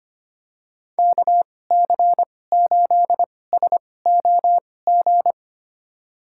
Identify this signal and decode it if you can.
Morse code, heard as a keyed tone, about 25 wpm, spaced as for 18 wpm, 700 Hz. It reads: KC8SOG
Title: USA Call Signs - 2x Character Spacing - 25wpm